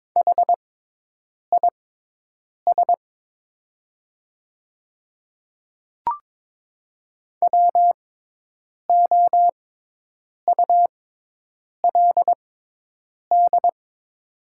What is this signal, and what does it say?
Morse code, heard as a keyed tone, about 22 wpm, spaced as for 8 wpm, 700 Hz. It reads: HIS WOULD
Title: 6x Character Spacing - Top 200 Words - 22wpm